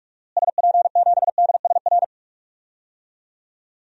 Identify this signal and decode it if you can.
Morse code, heard as a keyed tone, about 46 wpm, 700 Hz. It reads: SP7DSR
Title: DXCC-100 Call Signs - 6 Characters - Repeated 3x - 45wpm